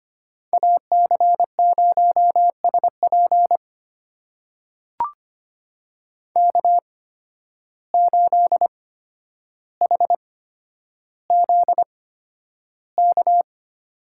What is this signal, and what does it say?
Morse code, heard as a keyed tone, about 25 wpm, spaced as for 11 wpm, 700 Hz. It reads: AC0SP K8HZK
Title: USA Call Signs - 8x Character Spacing - 25wpm